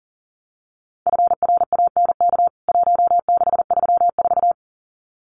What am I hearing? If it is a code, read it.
Morse code, heard as a keyed tone, about 40 wpm, 700 Hz. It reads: FRANK 1634
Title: CWT Exchange 40wpm